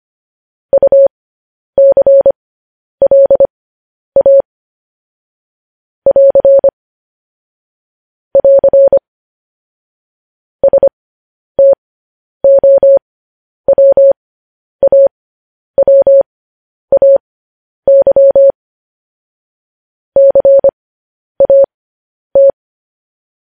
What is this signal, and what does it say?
Morse code, heard as a keyed tone, about 25 wpm, spaced as for 10 wpm, 550 Hz. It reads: UCLA <AR> <AR> STOWAWAY CAT